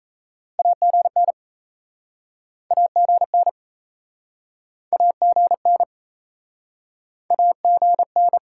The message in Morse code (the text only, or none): AGN AGN AGN AGN